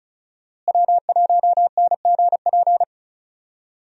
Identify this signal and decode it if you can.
Morse code, heard as a keyed tone, about 36 wpm, 700 Hz. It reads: W1NGP
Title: USA Call Signs - 5 Characters - Repeated 3x - 35wpm